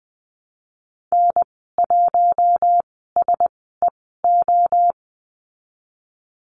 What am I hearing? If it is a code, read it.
Morse code, heard as a keyed tone, about 19 wpm, spaced as for 14 wpm, 700 Hz. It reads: N1SEO
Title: 048 Lesson 9g - Number 4 - Call Signs 20wpm